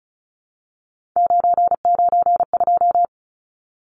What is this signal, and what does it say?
Morse code, heard as a keyed tone, about 34 wpm, 700 Hz. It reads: 992
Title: Numbers - 3-Digit 35wpm